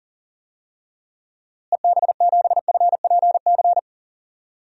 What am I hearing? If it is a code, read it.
Morse code, heard as a keyed tone, about 40 wpm, 700 Hz. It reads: EB7FPC
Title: DXCC-100 Call Signs - 6 Characters - Repeated 3x - 40wpm